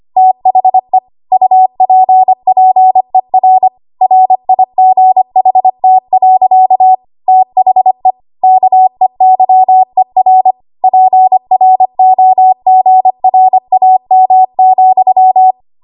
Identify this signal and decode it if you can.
Morse code, heard as a keyed tone, about 25 wpm, 750 Hz. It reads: THE UPPER RIGHT. THE KEYER PROGRAM,
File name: Archive / 25 WPM / 25WPM